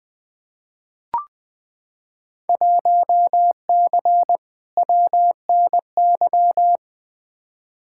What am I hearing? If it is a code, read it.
Morse code, heard as a keyed tone, about 20 wpm, 700 Hz. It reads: 1C WNY
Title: Field Day Exchanges Encore - 20wpm